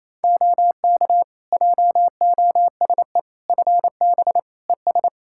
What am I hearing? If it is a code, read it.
Morse code, heard as a keyed tone, about 28 wpm, 700 Hz. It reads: OK JOSE FB ES
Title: K1USN SST QSO Practice 28wpm